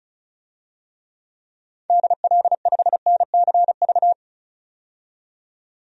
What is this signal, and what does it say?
Morse code, heard as a keyed tone, about 36 wpm, 700 Hz. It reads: DL5NCV
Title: DXCC-100 Call Signs - 6 Characters - 35wpm